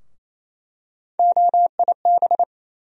Morse code, heard as a keyed tone, about 28 wpm, 700 Hz. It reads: OIB
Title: USA Call Sign Postfixes - 3 Letters 28wpm